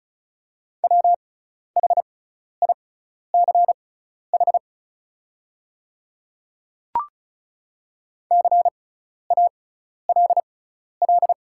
Morse code, heard as a keyed tone, about 35 wpm, spaced as for 12 wpm, 700 Hz. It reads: WHICH CALL
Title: 6x Character Spacing - Top 100 Words - 35wpm